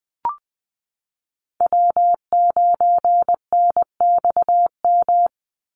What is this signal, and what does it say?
Morse code, heard as a keyed tone, about 20 wpm, 700 Hz. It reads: W9NXM
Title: Call Signs Repeated 3x - 20wpm